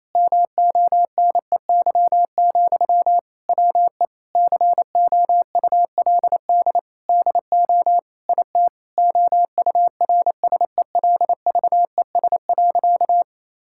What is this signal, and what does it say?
Morse code, heard as a keyed tone, about 28 wpm, 700 Hz. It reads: MONEY, WE COULD DO IT OURSELVES.
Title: Sentences from Top 1600 Words 28wpm